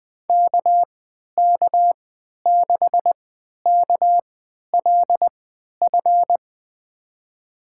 Morse code, heard as a keyed tone, about 20 wpm, spaced as for 11 wpm, 700 Hz. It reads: KK6KLF
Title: USA Call Signs - 6 Characters - 3x Character Spacing - 20wpm